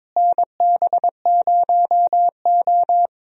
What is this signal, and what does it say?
Morse code, heard as a keyed tone, about 22 wpm, 700 Hz. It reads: NB0O
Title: USA Call Signs - 4 Characters - Repeated 2x - 22wpm